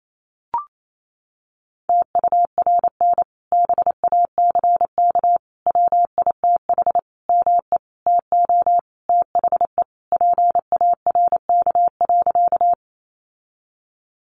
Morse code, heard as a keyed tone, about 28 wpm, 700 Hz. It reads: TURN BACK WITH ME TO THE PARK.
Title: Sentences from Top 1300 Words 28wpm